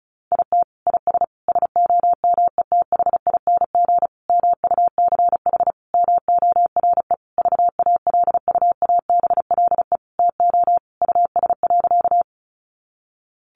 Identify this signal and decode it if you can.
Morse code, heard as a keyed tone, about 34 wpm, 700 Hz. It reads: IT IS SOMETHING MUCH MORE VALUABLE TO US.
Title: Sentences from Top 1100 Words 35wpm